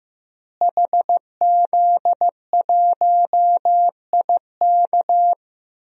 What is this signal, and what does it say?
Morse code, heard as a keyed tone, about 15 wpm, 700 Hz. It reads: HZ1IK
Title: DXCC-100 Call Signs - 5 Characters - Repeated 3x - 15wpm